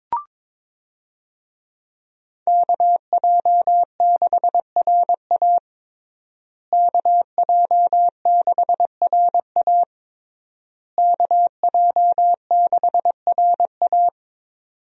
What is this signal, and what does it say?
Morse code, heard as a keyed tone, about 22 wpm, 700 Hz. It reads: KJ6RA KJ6RA KJ6RA
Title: USA Call Signs - 5 Characters - Repeated 3x - 22wpm